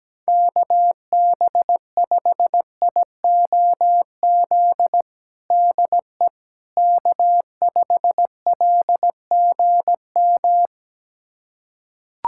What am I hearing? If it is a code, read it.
Morse code, heard as a keyed tone, about 17 wpm, 700 Hz. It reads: KB5IOZ DE K5LGM
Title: USA Callsign DE Callsign 17wpm